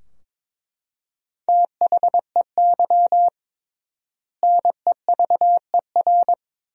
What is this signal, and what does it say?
Morse code, heard as a keyed tone, about 22 wpm, 700 Hz. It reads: THEY NEVER
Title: 3x Word-Spacing - Sentences from Top 300 Words - 22wpm